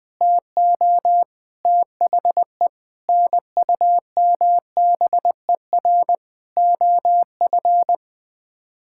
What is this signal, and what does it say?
Morse code, heard as a keyed tone, about 20 wpm, 700 Hz. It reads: TO THE NUMBER OF
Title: Sets of 8 Words - 20wpm